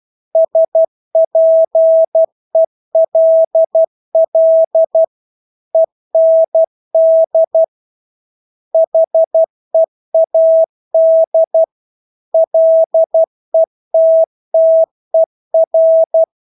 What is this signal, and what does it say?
Morse code, heard as a keyed tone, about 12 wpm, 650 Hz. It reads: SPELL END HEAD LETTER
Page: Engelske ord 12 wpm